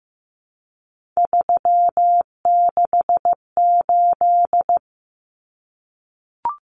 Morse code, heard as a keyed tone, about 15 wpm, 700 Hz. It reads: 368
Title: Numbers - 3-Digit 15wpm